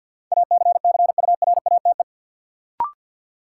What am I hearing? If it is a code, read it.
Morse code, heard as a keyed tone, about 53 wpm, 700 Hz. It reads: ACCURATE
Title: Sentences from Top 1200 Words - New Word Drill Alt - 50wpm